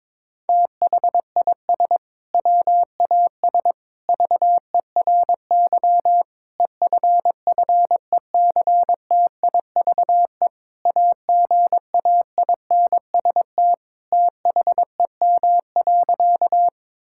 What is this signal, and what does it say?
Morse code, heard as a keyed tone, about 22 wpm, 700 Hz. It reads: THIS WAS VERY EFFECTIVE AGAINST THEM.
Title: Sentences from Top 1000 Words - Review Part 6 - 22wpm